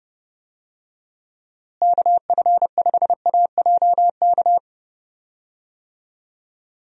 Morse code, heard as a keyed tone, about 30 wpm, 700 Hz. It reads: KF5AJK
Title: USA Call Signs - 6 Characters - Repeated 2x - 30wpm